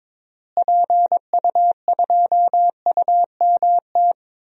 Morse code, heard as a keyed tone, about 22 wpm, 700 Hz. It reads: PU2UMT
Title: DXCC-100 Call Signs - 6 Characters - Repeated 3x - 22wpm